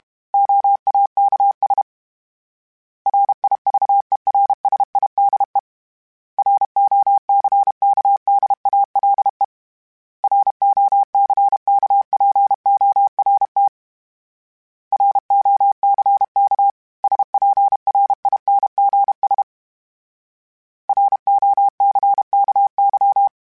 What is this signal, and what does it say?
Morse code, heard as a keyed tone, about 32 wpm, 800 Hz. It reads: OAKS RIVERSIDE ROCKDALE ROCKPORT ROCK SPRINGS ROCKY
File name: fromSMs / FromSMs mp3s / Cities@30